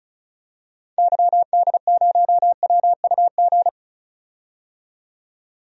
Morse code, heard as a keyed tone, about 36 wpm, 700 Hz. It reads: YD0WUG
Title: DXCC-100 Call Signs - 6 Characters - 35wpm